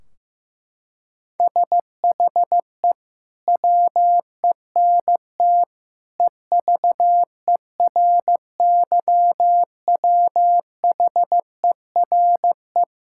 Morse code, heard as a keyed tone, about 15 wpm, 700 Hz. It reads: SHE WENT EVERYWHERE
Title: Sentences from Top 1600 Words 15wpm